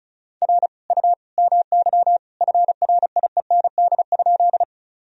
Morse code, heard as a keyed tone, about 34 wpm, 700 Hz. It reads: R U MY FRIEND?
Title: Sentences from Top 300 Words and Abbreviations 35wpm